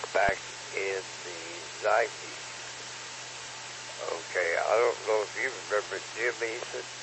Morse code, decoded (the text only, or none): none